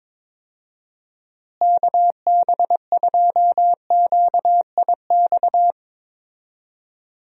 Morse code, heard as a keyed tone, about 22 wpm, 700 Hz. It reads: KB2QIX
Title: USA Call Signs - 6 Characters - Repeated 3x - 22wpm